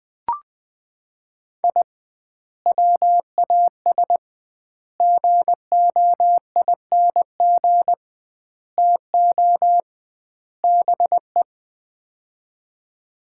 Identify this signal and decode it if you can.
Morse code, heard as a keyed tone, about 21 wpm, 700 Hz. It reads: I WAS GOING TO BE
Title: 2x Word-Spacing - Sets of 5 Words - 20wpm